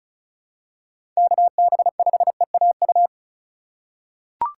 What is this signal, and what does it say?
Morse code, heard as a keyed tone, about 36 wpm, 700 Hz. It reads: KB5EAU
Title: USA Call Signs - 6 Characters - 4x Character Spacing - 35wpm